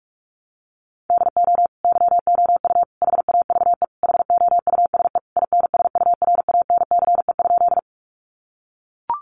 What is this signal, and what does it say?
Morse code, heard as a keyed tone, about 44 wpm, 700 Hz. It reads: DO YOU HAVE HOUSE INSURANCE?
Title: Sentences from Top 1100 Words 45wpm